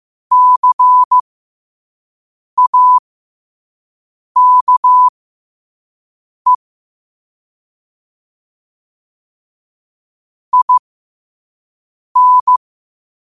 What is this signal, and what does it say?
Morse code, heard as a keyed tone, about 15 wpm, spaced as for 5 wpm, 1000 Hz. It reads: CAKE IN
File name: qotdfiles / 5wpm